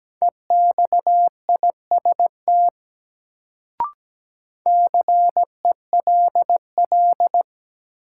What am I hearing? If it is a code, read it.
Morse code, heard as a keyed tone, about 17 wpm, 700 Hz. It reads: EXIST CELL
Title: Sentences from Top 1100 Words - New Word Drill Alt - 17wpm